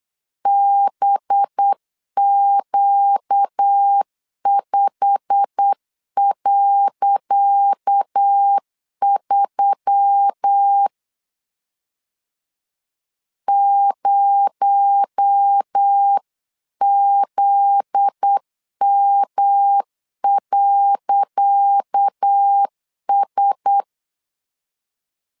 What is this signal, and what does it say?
Morse code, heard as a keyed tone, about 8 wpm, 800 Hz. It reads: BQ5.3 0ZM.S
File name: LEZIONI / 10 SIGNS / 10D COMA 8wpm 3749